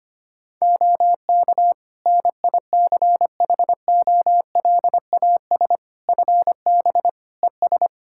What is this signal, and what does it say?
Morse code, heard as a keyed tone, about 25 wpm, 700 Hz. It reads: OK NICHOLAS FB ES
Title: Practice QSO - Exchange 2 - Info - 25wpm